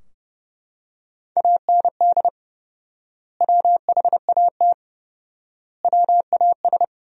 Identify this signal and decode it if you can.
Morse code, heard as a keyed tone, about 30 wpm, 700 Hz. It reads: AND WHAT WAS
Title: 4x Word-Spacing - Sentences from Top 100 Words - 30wpm